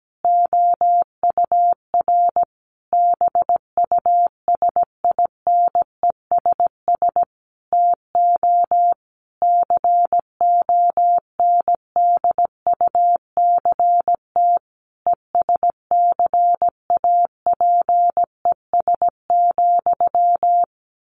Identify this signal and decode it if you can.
Morse code, heard as a keyed tone, about 17 wpm, 700 Hz. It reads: OUR BUSINESS TO CONDUCT ESCAPES,
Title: A Sound Of Thunder 17wpm